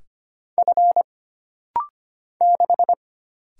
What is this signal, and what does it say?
Morse code, heard as a keyed tone, about 25 wpm, 700 Hz. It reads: F 6
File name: Single Character (Letter:Number:Punctuation:Prosign) - Rapid-Fire - 25wpm